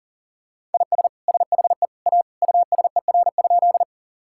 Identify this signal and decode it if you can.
Morse code, heard as a keyed tone, about 37 wpm, 700 Hz. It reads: IS SHE A USER?
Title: Sentences from Top 1000 Words - Review Part 5 - 40wpm